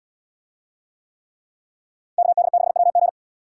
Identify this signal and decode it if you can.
Morse code, heard as a keyed tone, about 71 wpm, 700 Hz. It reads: KD6WG